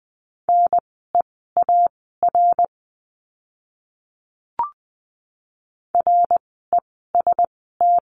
Morse code, heard as a keyed tone, about 20 wpm, spaced as for 14 wpm, 700 Hz. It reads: NEAR REST
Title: 054 Lesson 10f - Letter r - Words 20wpm